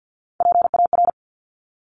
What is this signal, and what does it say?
Morse code, heard as a keyed tone, about 50 wpm, 700 Hz. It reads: PAR
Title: Top Three Letter Combinations in Words 50wpm